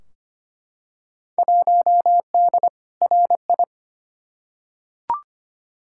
Morse code, heard as a keyed tone, about 25 wpm, 700 Hz. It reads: 1D RI
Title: Field Day Exchanges - 25wpm